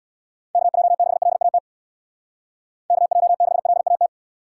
Morse code, heard as a keyed tone, about 62 wpm, 700 Hz. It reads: KJ6LAT KJ6LAT
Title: Speed Racing - USA 2x3 Call Signs 50wpm